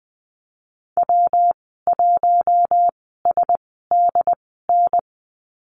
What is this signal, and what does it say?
Morse code, heard as a keyed tone, about 20 wpm, spaced as for 15 wpm, 700 Hz. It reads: W1SDN
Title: 128 Lesson 21g - Number 6 - Call Signs 20wpm